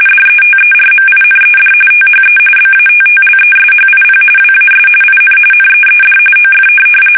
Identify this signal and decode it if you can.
RTTY, about 45.45 baud. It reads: DETERMINED THE BATTLESHIP WAS IN NEED OF
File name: fsk850 / USS-Massachusetts